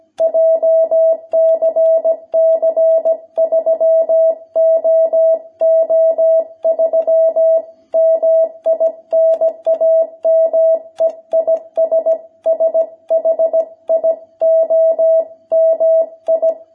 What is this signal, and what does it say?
Morse code, heard as a keyed tone, about 17 wpm, 660 Hz. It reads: JCC 3OO3 MI NAME IS SHIOMI
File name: jh8cbh / 2021cw